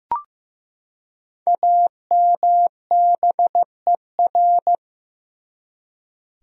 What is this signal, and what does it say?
Morse code, heard as a keyed tone, about 15 wpm, 700 Hz. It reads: AMBER
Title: USA Common Names - Max 5 Letters - 15wpm